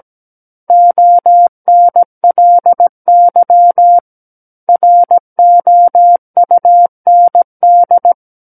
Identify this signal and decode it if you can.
Morse code, heard as a keyed tone, about 17 wpm, 700 Hz. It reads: ONLY ROUND
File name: Book One / 08-Chapter 8 - 17 WPM